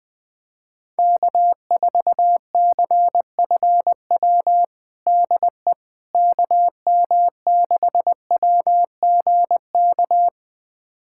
Morse code, heard as a keyed tone, about 20 wpm, 700 Hz. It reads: K4CFW DE KM6WGK
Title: USA Callsign DE Callsign 20wpm